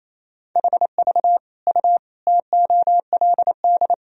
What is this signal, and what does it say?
Morse code, heard as a keyed tone, about 28 wpm, 700 Hz. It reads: HV U TOLD
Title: Sentences from Top 400 Words and Abbreviations 28wpm